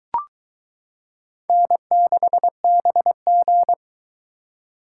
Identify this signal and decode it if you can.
Morse code, heard as a keyed tone, about 21 wpm, 700 Hz. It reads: N6BG